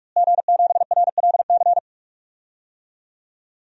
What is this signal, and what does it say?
Morse code, heard as a keyed tone, about 41 wpm, 700 Hz. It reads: G7RLC
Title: DXCC-100 Call Signs - 5 Characters - Repeated 2x - 45wpm